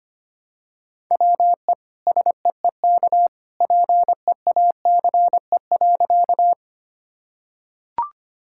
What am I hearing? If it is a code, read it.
Morse code, heard as a keyed tone, about 25 wpm, 700 Hz. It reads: WE SEEK PEACE.
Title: Sentences from Top 1600 Words 25wpm